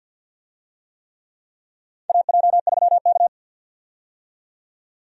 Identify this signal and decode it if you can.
Morse code, heard as a keyed tone, about 49 wpm, 700 Hz. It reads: AJ3K
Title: USA Call Signs - 4 Characters - 50wpm